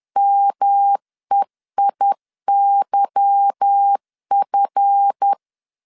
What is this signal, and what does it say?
Morse code, heard as a keyed tone, about 10 wpm, 800 Hz. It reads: MEIYF
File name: LEZIONI / 8 YJ45 / 8 YJ4 10wpm 2633